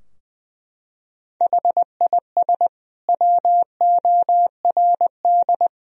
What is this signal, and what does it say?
Morse code, heard as a keyed tone, about 21 wpm, 700 Hz. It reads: HIS WORD
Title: Sentences from Top 1000 Words - Review Part 3 - 20wpm